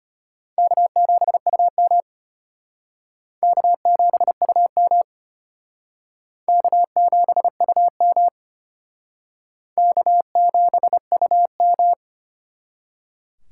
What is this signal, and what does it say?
Morse code, heard as a keyed tone, about 32 wpm, 700 Hz. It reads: K7UM K7UM K7UM K7UM